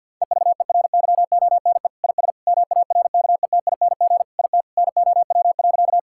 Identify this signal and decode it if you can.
Morse code, heard as a keyed tone, about 50 wpm, 700 Hz. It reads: EVERYONE IS MARKETING IT NOW.
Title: Sentences from Top 1000 Words - Review Part 5 - 50wpm